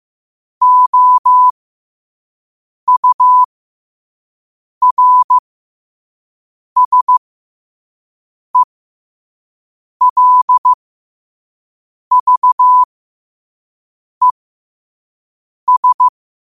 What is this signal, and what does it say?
Morse code, heard as a keyed tone, about 15 wpm, spaced as for 5 wpm, 1000 Hz. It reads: OURSELVES
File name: qotdfiles / 5wpm